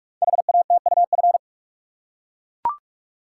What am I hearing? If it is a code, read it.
Morse code, heard as a keyed tone, about 44 wpm, 690 Hz. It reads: HATUF
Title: Code Groups - 5 Letters - 45wpm